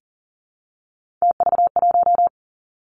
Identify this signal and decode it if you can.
Morse code, heard as a keyed tone, about 38 wpm, 700 Hz. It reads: TV1
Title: Three Challenging Letter-Number Combinations 40wpm